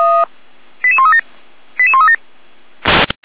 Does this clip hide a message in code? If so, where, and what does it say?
Yes: five-tone selective call (ZVEI) from 0.8 to 2.2 s, heard as quick runs of five tones, tones of about 70 ms. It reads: 88137 88137